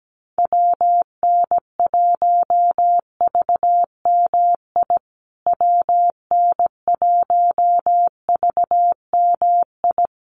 Morse code, heard as a keyed tone, about 17 wpm, 700 Hz. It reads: WN1VMI WN1VMI
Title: USA Call Signs Repeated 3x 17wpm